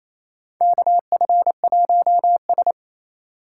Morse code, heard as a keyed tone, about 27 wpm, 700 Hz. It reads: KF1S
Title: USA Call Signs - 2x Character Spacing - 28wpm